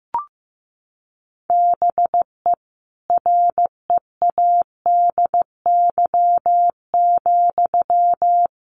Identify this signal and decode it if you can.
Morse code, heard as a keyed tone, about 15 wpm, 700 Hz. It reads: BE READY,
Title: Sentences from Top 1400 Words 15wpm